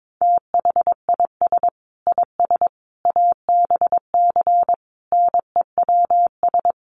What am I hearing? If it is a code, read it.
Morse code, heard as a keyed tone, about 22 wpm, 700 Hz. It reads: THIS IS ABC NEWS
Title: Sets of 5 Words 22wpm